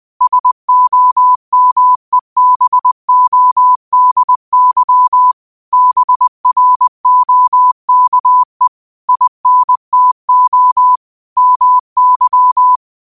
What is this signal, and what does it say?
Morse code, heard as a keyed tone, about 20 wpm, 1000 Hz. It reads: SOMEBODY BROKE INTO MY
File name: 2025-06-25-20wpm